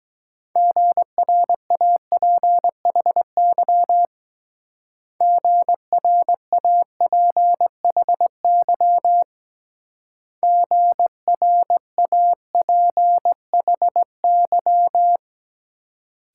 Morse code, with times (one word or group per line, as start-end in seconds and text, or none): 0.6-4.1 GRAPHY
5.2-9.2 GRAPHY
10.4-15.2 GRAPHY